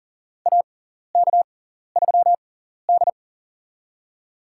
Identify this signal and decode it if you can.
Morse code, heard as a keyed tone, about 40 wpm, spaced as for 14 wpm, 700 Hz. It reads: AK3D
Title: USA Call Signs - 4 Characters - 6x Character Spacing - 40wpm